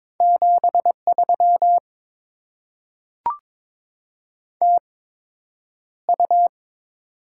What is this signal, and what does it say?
Morse code, heard as a keyed tone, about 22 wpm, spaced as for 9 wpm, 700 Hz. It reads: 73 TU
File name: Top 20 QSO Elements - 8x Character Spacing - 22wpm